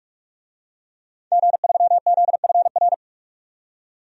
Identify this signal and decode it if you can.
Morse code, heard as a keyed tone, about 44 wpm, 700 Hz. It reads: G3ZFR
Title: DXCC-100 Call Signs - 5 Characters - 45wpm